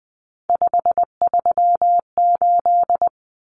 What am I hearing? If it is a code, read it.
Morse code, heard as a keyed tone, about 20 wpm, 700 Hz. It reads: 538
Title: Common Cut Numbers - 3-Digit 20wpm